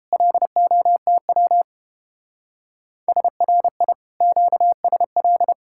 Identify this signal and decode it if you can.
Morse code, heard as a keyed tone, about 29 wpm, 700 Hz. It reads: LOTW SRI QSL